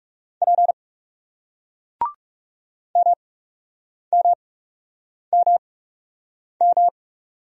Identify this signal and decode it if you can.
Morse code, heard as a keyed tone, about 40 wpm, spaced as for 8 wpm, 700 Hz. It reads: P MMMM